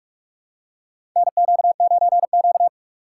Morse code, heard as a keyed tone, about 46 wpm, 700 Hz. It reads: NQ9Q
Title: USA Call Signs - 4 Characters - 3x Character Spacing - 45wpm